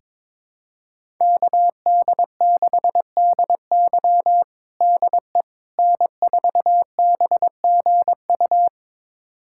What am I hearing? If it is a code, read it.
Morse code, heard as a keyed tone, about 22 wpm, 700 Hz. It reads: KD6DY DE N4BGU
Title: USA Callsign DE Callsign - 5 Characters - 22wpm